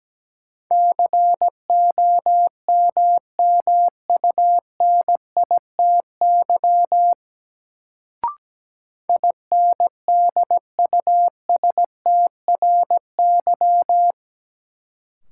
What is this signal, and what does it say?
Morse code, heard as a keyed tone, about 17 wpm, 700 Hz. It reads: COMMUNITY INDUSTRY
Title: Sentences from Top 1000 Words - Review Part 4 - New Word Drill - 17wpm